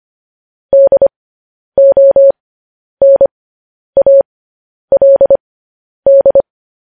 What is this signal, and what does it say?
Morse code, heard as a keyed tone, about 25 wpm, spaced as for 10 wpm, 550 Hz. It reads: DONALD
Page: news-headlines-in-morse-code-at-10-wpm